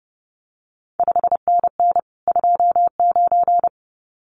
Morse code, heard as a keyed tone, about 30 wpm, 700 Hz. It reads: 5NN 29
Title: CQ WW DX Contest Exchange Encore - 30wpm